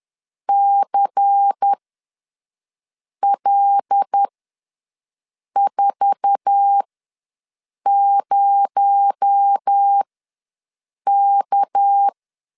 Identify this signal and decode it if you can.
Morse code, heard as a keyed tone, about 10 wpm, spaced as for 5 wpm, 800 Hz. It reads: CL40K
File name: LEZIONI / 9 28 ws 10wpm 4124